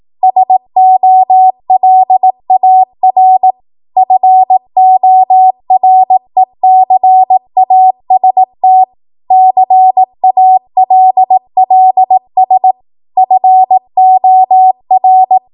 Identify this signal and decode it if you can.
Morse code, heard as a keyed tone, about 18 wpm, 750 Hz. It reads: SOLAR FORECAST CALLS FOR
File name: Morse / 18WPM